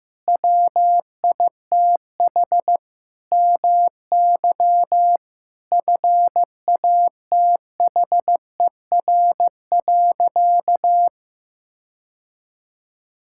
Morse code, heard as a keyed tone, about 16 wpm, 700 Hz. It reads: WITH MY FATHER.
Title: Sentences from Top 1000 Words - Review Part 5 - 15wpm